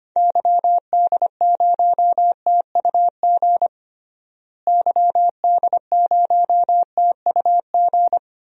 Morse code, heard as a keyed tone, about 25 wpm, 700 Hz. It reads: YD0TUG YD0TUG
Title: DXCC-100 Call Signs - 6 Characters - Repeated 2x - 25wpm